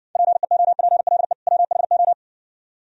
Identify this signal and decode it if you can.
Morse code, heard as a keyed tone, about 57 wpm, 700 Hz. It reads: PEOPLE WHO